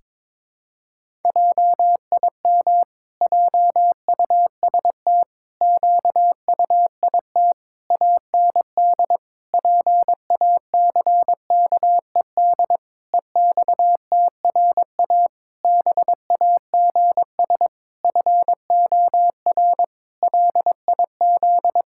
Morse code, heard as a keyed tone, about 22 wpm, 700 Hz. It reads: JIM JUST QUIT AND PACKED EXTRA BAGS FOR LIZ
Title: Pangram Challenge v2 - 22wpm